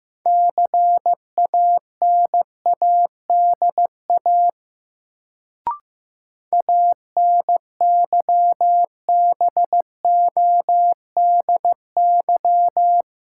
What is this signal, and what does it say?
Morse code, heard as a keyed tone, about 15 wpm, 700 Hz. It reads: CANADA ANYBODY
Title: Sets of 5 Words - New Word Drill - 15wpm